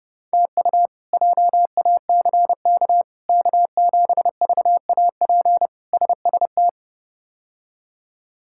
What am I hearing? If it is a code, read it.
Morse code, heard as a keyed tone, about 29 wpm, 700 Hz. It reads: TU JACK K7VAP SST
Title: K1USN SST QSO Practice - 30wpm